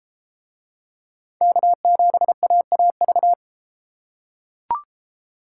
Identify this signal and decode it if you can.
Morse code, heard as a keyed tone, about 31 wpm, 700 Hz. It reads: K7AAV